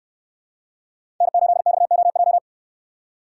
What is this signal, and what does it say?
Morse code, heard as a keyed tone, about 69 wpm, 700 Hz. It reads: N8XOJ